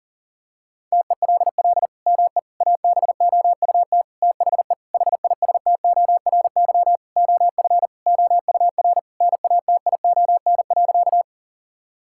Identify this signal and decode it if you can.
Morse code, heard as a keyed tone, about 40 wpm, 700 Hz. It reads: TELL ME ABOUT THE HISTORY OF OUR NATION.
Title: Sentences from Top 1000 Words - Review Part 6 - 40wpm